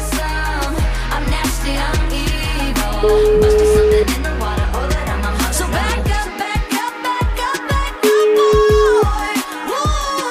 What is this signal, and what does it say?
Ringback tone (Europe, 425 Hz), with noise.